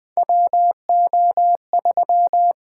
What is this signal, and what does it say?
Morse code, heard as a keyed tone, about 20 wpm, 700 Hz. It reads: WO3
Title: USA Call Signs - 8x Character Spacing - 20wpm